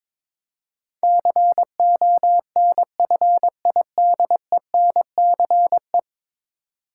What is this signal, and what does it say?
Morse code, heard as a keyed tone, about 21 wpm, 700 Hz. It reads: CONFIDENCE
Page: morse-code-ninja-practice-22wpm